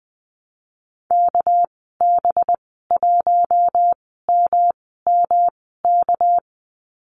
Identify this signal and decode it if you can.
Morse code, heard as a keyed tone, about 20 wpm, spaced as for 15 wpm, 700 Hz. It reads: KB1MMK
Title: 207 Lesson 33g - Letter j - Call Signs 20wpm